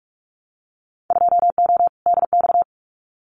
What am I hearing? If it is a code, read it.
Morse code, heard as a keyed tone, about 44 wpm, 700 Hz. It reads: 2O DX
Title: Winter Field Day Exchanges Encore 45wpm